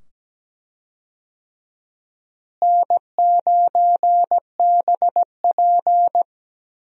Morse code, heard as a keyed tone, about 17 wpm, 700 Hz. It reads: N9BP
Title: USA Call Signs - 4 Characters - Repeated 2x - 17wpm